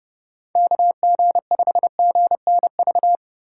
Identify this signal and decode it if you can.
Morse code, heard as a keyed tone, about 30 wpm, 700 Hz. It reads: KG5GNV
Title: USA Call Signs - 6 Characters - 4x Character Spacing - 30wpm